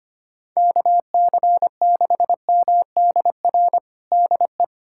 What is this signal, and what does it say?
Morse code, heard as a keyed tone, about 25 wpm, 700 Hz. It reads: KC6MDR DE
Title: USA Callsign DE Callsign - 6 Characters - 25wpm